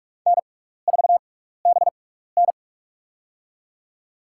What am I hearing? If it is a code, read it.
Morse code, heard as a keyed tone, about 46 wpm, spaced as for 16 wpm, 700 Hz. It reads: N4BN
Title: USA Call Signs - 4 Characters - 6x Character Spacing - 45wpm